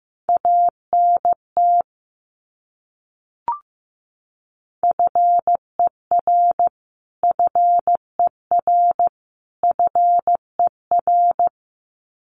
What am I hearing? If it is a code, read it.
Morse code, heard as a keyed tone, about 15 wpm, 700 Hz. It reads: ANT FER FER FER
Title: Abbreviations Repeated 3x - 15wpm